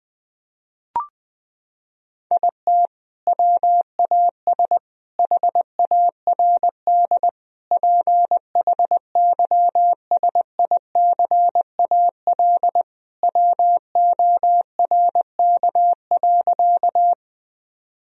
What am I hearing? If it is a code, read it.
Morse code, heard as a keyed tone, about 21 wpm, 700 Hz. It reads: IT WAS HARD PHYSICAL WORK.
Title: Sentences from Top 1000 Words - Review Part 4 - 20wpm